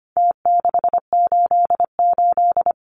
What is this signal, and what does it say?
Morse code, heard as a keyed tone, about 25 wpm, 700 Hz. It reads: T688
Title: Numbers - Contest Style - 4-Digit 25wpm